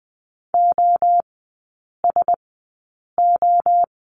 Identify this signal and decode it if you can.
Morse code, heard as a keyed tone, about 20 wpm, spaced as for 8 wpm, 700 Hz. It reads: OSO
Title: 074 Lesson 13e - Letter L - Three Characters 20wpm